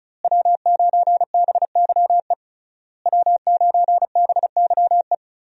Morse code, heard as a keyed tone, about 34 wpm, 700 Hz. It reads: W9BYE W9BYE
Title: USA Call Signs - 5 Characters - Repeated 3x - 35wpm